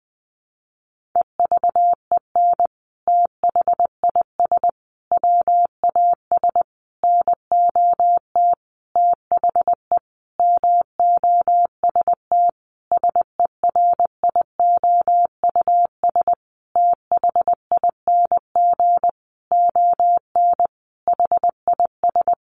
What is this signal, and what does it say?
Morse code, heard as a keyed tone, about 20 wpm, 700 Hz. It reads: EVEN THIS WAS NOT THE MOST SERIOUS THING ON HIS
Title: Sentences from Top 1100 Words 20wpm